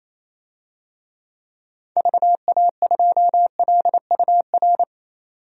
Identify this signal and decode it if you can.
Morse code, heard as a keyed tone, about 28 wpm, 700 Hz. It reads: VA2LUR
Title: DXCC-100 Call Signs - 6 Characters - 28wpm